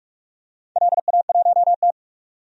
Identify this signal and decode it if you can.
Morse code, heard as a keyed tone, about 46 wpm, 700 Hz. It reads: LA1T
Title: DXCC-100 Call Signs - 4 Characters - 45wpm